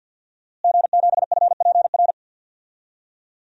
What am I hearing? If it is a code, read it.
Morse code, heard as a keyed tone, about 48 wpm, 700 Hz. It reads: G7FPR
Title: DXCC-100 Call Signs - 5 Characters - Repeated 2x - 50wpm